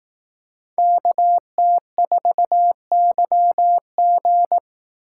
Morse code, heard as a keyed tone, about 18 wpm, 700 Hz. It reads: KT4YG
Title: Speed Racing - USA 2x2 Call Signs 15wpm